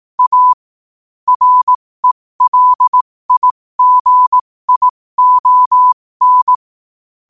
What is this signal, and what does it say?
Morse code, heard as a keyed tone, about 18 wpm, spaced as for 15 wpm, 1000 Hz. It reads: A RELIGION
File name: qotdfiles / 15wpm